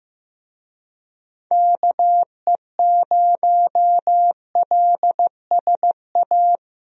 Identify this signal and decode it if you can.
Morse code, heard as a keyed tone, about 15 wpm, 700 Hz. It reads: KE0LSA
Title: USA Call Signs - 6 Characters - Repeated 2x - 15wpm